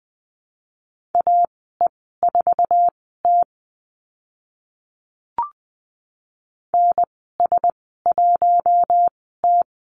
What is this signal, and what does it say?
Morse code, heard as a keyed tone, about 20 wpm, spaced as for 14 wpm, 700 Hz. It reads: AE4T NS1T
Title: 062 Lesson 11g - Letter h - Call Signs 20wpm